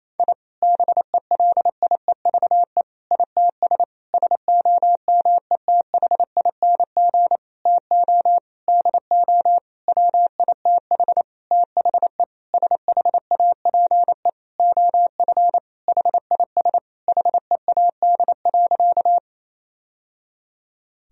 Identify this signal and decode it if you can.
Morse code, heard as a keyed tone, about 28 wpm, 700 Hz. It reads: I BELIEVE ITS SOMETHING TO DO WITH THE SHAPE OF HIS HEAD.
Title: Sentences from Top 1000 Words - Review Part 3 - 28wpm